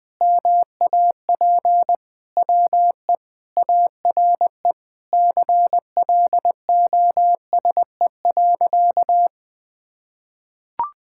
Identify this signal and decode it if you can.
Morse code, heard as a keyed tone, about 20 wpm, 700 Hz. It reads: MAP WE ARE CLOSE.
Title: Sentences with 2-Word Phrases - 20wpm